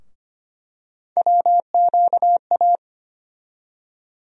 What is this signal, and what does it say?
Morse code, heard as a keyed tone, about 25 wpm, 700 Hz. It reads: WQA
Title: USA Call Sign Postfixes - 3 Letters 25wpm